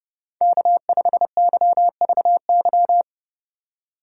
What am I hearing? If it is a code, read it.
Morse code, heard as a keyed tone, about 30 wpm, 700 Hz. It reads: K5YVY
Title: K1USN SST QSO Practice - 30wpm